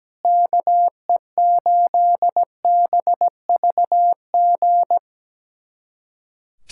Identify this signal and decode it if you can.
Morse code, heard as a keyed tone, about 17 wpm, 700 Hz. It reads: KE8BVG
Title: USA Callsign DE Callsign - 6 Characters - 17wpm